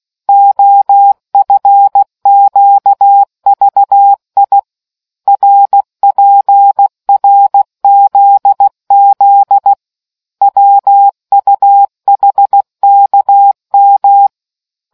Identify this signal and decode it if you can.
Morse code, heard as a keyed tone, about 16 wpm, 800 Hz. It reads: OFQVI RPRZZ WUHKM